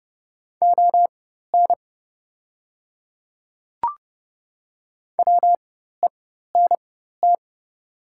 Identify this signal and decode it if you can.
Morse code, heard as a keyed tone, about 28 wpm, spaced as for 14 wpm, 700 Hz. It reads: ON WENT
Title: 4x Character Spacing - Top 200 Words - 30wpm